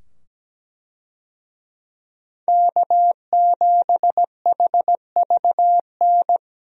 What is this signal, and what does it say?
Morse code, heard as a keyed tone, about 17 wpm, 700 Hz. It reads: K7HVN
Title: USA Call Signs - 5 Characters - 17wpm